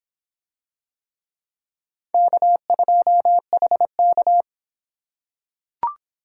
Morse code, heard as a keyed tone, about 26 wpm, 700 Hz. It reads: K2HK